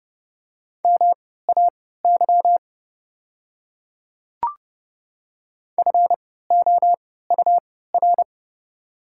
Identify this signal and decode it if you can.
Morse code, heard as a keyed tone, about 30 wpm, spaced as for 17 wpm, 690 Hz. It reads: MAY FOUR
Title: 3x Character Spacing - Top 300 Words - 30wpm